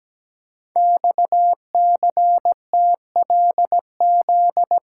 Morse code, heard as a keyed tone, about 17 wpm, 700 Hz. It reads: XCTLZ
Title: Code Groups - 5 Letters - 17wpm